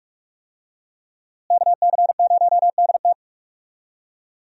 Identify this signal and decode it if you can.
Morse code, heard as a keyed tone, about 44 wpm, 700 Hz. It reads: KC0DT
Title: USA Call Signs - 5 Characters - Repeated 3x - 45wpm